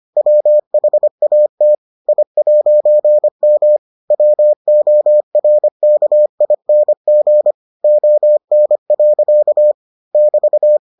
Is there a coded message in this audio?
Yes: Morse code from 0.2 to 10.8 s, heard as a keyed tone, about 25 wpm, 600 Hz. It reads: WHAT I'M WORKING ON. <BT>